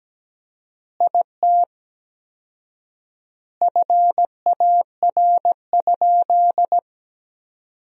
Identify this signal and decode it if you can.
Morse code, heard as a keyed tone, about 17 wpm, 700 Hz. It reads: IT FAR?
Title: 4x Word-Spacing - Sentences from Top 300 Words - 17wpm